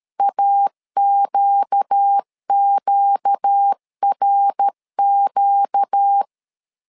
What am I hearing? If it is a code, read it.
Morse code, heard as a keyed tone, about 13 wpm, 800 Hz. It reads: AQQRQ
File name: LEZIONI / 4 Q 12wpm 1833